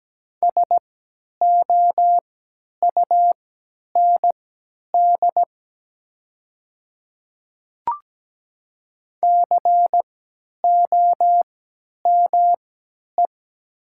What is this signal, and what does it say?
Morse code, heard as a keyed tone, about 17 wpm, spaced as for 10 wpm, 700 Hz. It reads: SOUND COME
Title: 3x Character Spacing - Top 100 Words - 17wpm